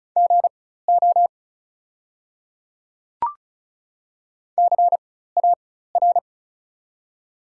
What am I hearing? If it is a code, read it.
Morse code, heard as a keyed tone, about 36 wpm, spaced as for 16 wpm, 700 Hz. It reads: GO CAR
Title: 4x Character Spacing - Top 300 Words - 35wpm